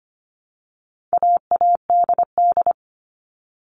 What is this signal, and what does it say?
Morse code, heard as a keyed tone, about 25 wpm, 700 Hz. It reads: AADD
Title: All Cut Numbers - 4-Digit 25wpm